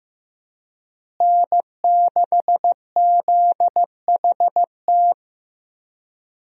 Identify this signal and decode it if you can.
Morse code, heard as a keyed tone, about 16 wpm, 700 Hz. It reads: N6ZHT
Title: USA Call Signs - 5 Characters - Repeated 3x - 15wpm